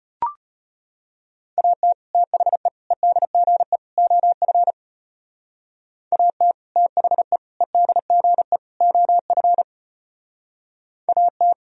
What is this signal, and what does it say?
Morse code, heard as a keyed tone, about 31 wpm, 700 Hz. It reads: AT THE EDGE OF AT THE EDGE OF AT